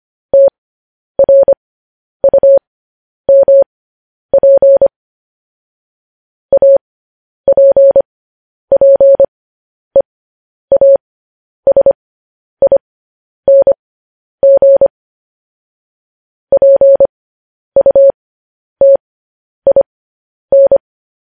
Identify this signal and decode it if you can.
Morse code, heard as a keyed tone, about 25 wpm, spaced as for 10 wpm, 550 Hz. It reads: TRUMP APPEASING PUTIN
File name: podcast / News headlines-1x-25x10